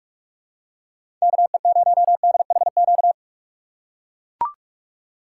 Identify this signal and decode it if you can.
Morse code, heard as a keyed tone, about 41 wpm, 700 Hz. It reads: KE0DHQ
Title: USA Call Signs - 6 Characters - 6x Character Spacing - 45wpm